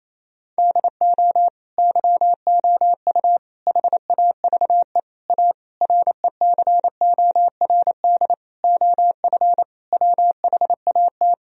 Morse code, heard as a keyed tone, about 28 wpm, 700 Hz. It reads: DO YOU HAVE A RECORD OF WHAT
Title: Sentences from Top 1000 Words - Review Part 3 - 28wpm